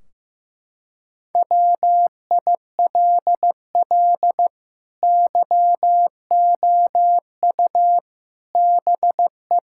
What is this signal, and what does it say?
Morse code, heard as a keyed tone, about 15 wpm, 700 Hz. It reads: WILL YOU BE
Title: Sentences from Top 1600 Words 15wpm